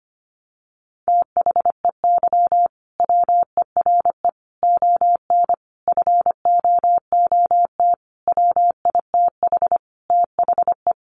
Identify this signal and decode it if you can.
Morse code, heard as a keyed tone, about 25 wpm, 700 Hz. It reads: THEY WERE ON FOOT WITH THE
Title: Sentences from Top 1500 Words 25wpm